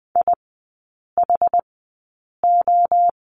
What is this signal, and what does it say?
Morse code, heard as a keyed tone, about 20 wpm, spaced as for 8 wpm, 690 Hz. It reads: IHO
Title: 060 Lesson 11e - Letter h - Three Characters 20wpm